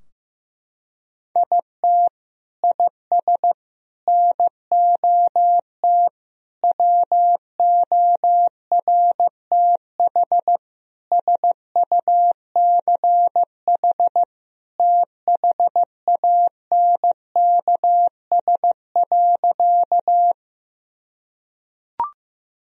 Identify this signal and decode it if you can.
Morse code, heard as a keyed tone, about 15 wpm, 700 Hz. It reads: IT IS NOT WORTH SUCH THANKS.
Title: Sentences from Top 1000 Words - Review Part 3 - 15wpm